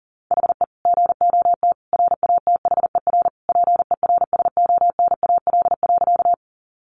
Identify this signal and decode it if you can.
Morse code, heard as a keyed tone, about 40 wpm, 700 Hz. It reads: HE GOT RATHER PERSONAL.
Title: Sentences from Top 700 Words 40wpm